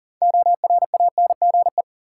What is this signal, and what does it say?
Morse code, heard as a keyed tone, about 40 wpm, 700 Hz. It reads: ORANGE
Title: Six Letter Words - 40wpm